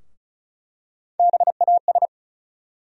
Morse code, heard as a keyed tone, about 36 wpm, 700 Hz. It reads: BAS
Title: USA Call Sign Postfixes - 3 Letters 35wpm